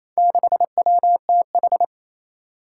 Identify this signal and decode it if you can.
Morse code, heard as a keyed tone, about 28 wpm, 700 Hz. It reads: 6WTH
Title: Four Challenging Letter-Number Combinations 28wpm